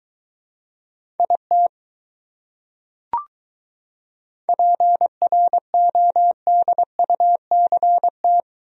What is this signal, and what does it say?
Morse code, heard as a keyed tone, about 22 wpm, 700 Hz. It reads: IT PRODUCT